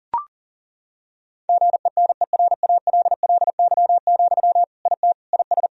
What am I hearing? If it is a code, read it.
Morse code, heard as a keyed tone, about 40 wpm, 700 Hz. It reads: GENERALLY, IT IS
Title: Sentences from Top 1000 Words - Review Part 4 - 40wpm